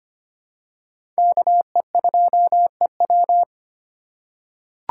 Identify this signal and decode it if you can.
Morse code, heard as a keyed tone, about 25 wpm, 700 Hz. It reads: KE2EW
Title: USA Call Signs - 5 Characters - 8x Character Spacing - 25wpm